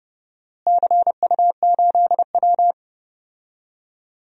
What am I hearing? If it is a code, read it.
Morse code, heard as a keyed tone, about 30 wpm, 700 Hz. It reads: CU8W
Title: DXCC-100 Call Signs - 4 Characters - Repeated 3x - 30wpm